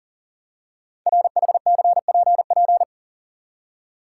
Morse code, heard as a keyed tone, about 40 wpm, 690 Hz. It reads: RHCPP
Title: Code Groups - 5 Letters - 40wpm